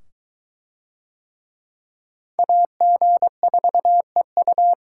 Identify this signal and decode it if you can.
Morse code, heard as a keyed tone, about 23 wpm, 700 Hz. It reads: AG4EU